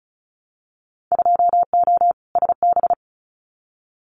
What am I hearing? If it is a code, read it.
Morse code, heard as a keyed tone, about 34 wpm, 700 Hz. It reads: 2O SB
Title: Winter Field Day Exchanges 35wpm